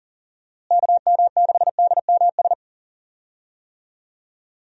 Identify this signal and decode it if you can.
Morse code, heard as a keyed tone, about 40 wpm, 700 Hz. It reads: KM6DMS
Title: USA Call Signs - 6 Characters - Repeated 3x - 40wpm